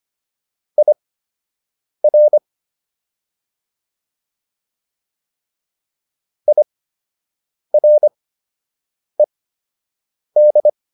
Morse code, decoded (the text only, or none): IR IRED